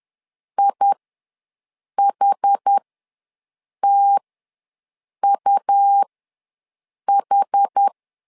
Morse code, heard as a keyed tone, about 10 wpm, spaced as for 6 wpm, 800 Hz. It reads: IHTUH
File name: LEZIONI / 2 UHDN / 2 UH ws 10wpm 2029